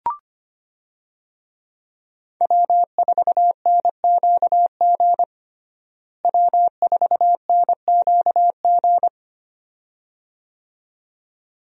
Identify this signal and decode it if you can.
Morse code, heard as a keyed tone, about 25 wpm, 700 Hz. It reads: W4NQG W4NQG
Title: USA Call Signs - 5 Characters - Repeated 2x - 25wpm